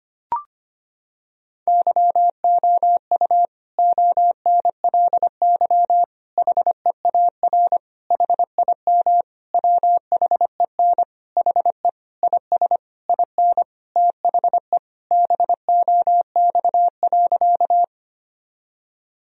Morse code, heard as a keyed tone, about 25 wpm, 700 Hz. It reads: YOU ONLY HEAR HIM WHEN HE IS IN THE BOX.
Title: Sentences from Top 1000 Words - Review Part 2 - 25wpm